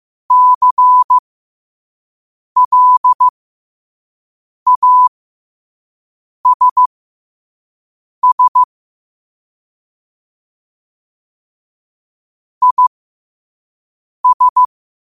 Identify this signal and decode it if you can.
Morse code, heard as a keyed tone, about 15 wpm, spaced as for 5 wpm, 1000 Hz. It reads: CLASS IS
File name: qotdfiles / 5wpm